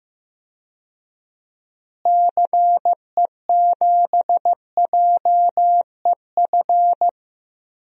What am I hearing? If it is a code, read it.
Morse code, heard as a keyed tone, about 15 wpm, 700 Hz. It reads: CE7JEF
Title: DXCC-100 Call Signs - 6 Characters - Repeated 2x - 15wpm